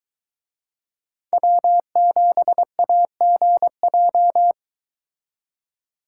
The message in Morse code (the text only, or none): W7AGJ